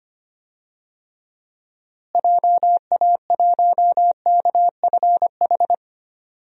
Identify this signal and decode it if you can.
Morse code, heard as a keyed tone, about 25 wpm, 700 Hz. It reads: JA1KFH
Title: DXCC-100 Call Signs - 6 Characters - Repeated 3x - 25wpm